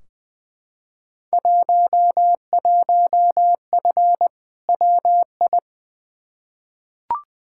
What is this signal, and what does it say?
Morse code, heard as a keyed tone, about 20 wpm, 700 Hz. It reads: 11F WI
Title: Field Day Exchanges Encore - 20wpm